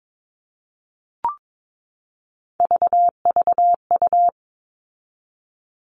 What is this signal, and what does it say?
Morse code, heard as a keyed tone, about 22 wpm, 700 Hz. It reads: VVU
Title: All Cut Numbers - 3-Digit 22wpm